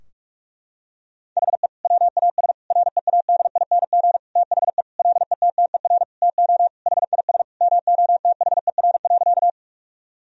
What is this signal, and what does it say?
Morse code, heard as a keyed tone, about 42 wpm, 700 Hz. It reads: HE WAS READING THE LETTER TO HIS MOTHER.
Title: Sentences from Top 1000 Words - Review Part 3 - 45wpm